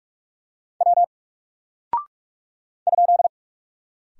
Morse code, heard as a keyed tone, about 44 wpm, 700 Hz. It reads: W ?
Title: Single Character (Letter:Number:Punctuation:Prosign) Alt 45wpm